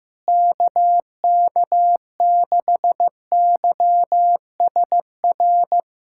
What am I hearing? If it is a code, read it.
Morse code, heard as a keyed tone, about 15 wpm, 700 Hz. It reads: KK6YSR
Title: USA Callsign DE Callsign - 6 Characters - 15wpm